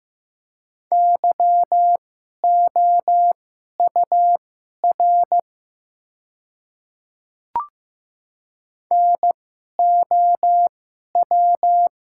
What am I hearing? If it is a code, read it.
Morse code, heard as a keyed tone, about 15 wpm, spaced as for 11 wpm, 700 Hz. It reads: YOUR NOW
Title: 2x Character Spacing - Top 300 Words - 15wpm